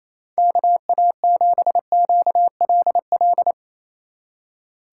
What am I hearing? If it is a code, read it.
Morse code, heard as a keyed tone, about 36 wpm, 700 Hz. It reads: KA7QLL